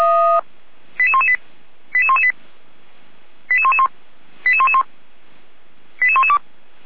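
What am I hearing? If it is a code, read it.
Five-tone selective call (ZVEI), heard as quick runs of five tones, tones of about 70 ms. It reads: 88118 88118 88111 88111 88112
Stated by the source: ZVEI five-tone selective call received on an SDR, from the Feuerwehr Orlamünde archive.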